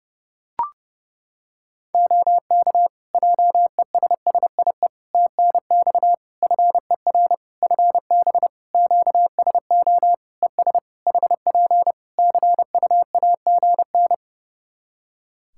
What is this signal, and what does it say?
Morse code, heard as a keyed tone, about 30 wpm, 700 Hz. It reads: OK JESSIE TNX FER FB QSO ES HP CUAGN
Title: Practice QSO - Exchange 4 - Ending - 30wpm